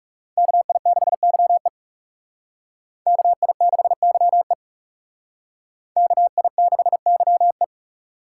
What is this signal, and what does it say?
Morse code, heard as a keyed tone, about 38 wpm, 700 Hz. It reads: KI6YE KI6YE KI6YE
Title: Speed Racing - USA 2x2 Call Signs 30wpm